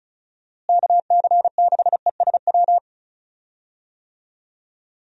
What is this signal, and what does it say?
Morse code, heard as a keyed tone, about 36 wpm, 700 Hz. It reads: KC6ESW
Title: USA Call Signs - 6 Characters - Repeated 2x - 35wpm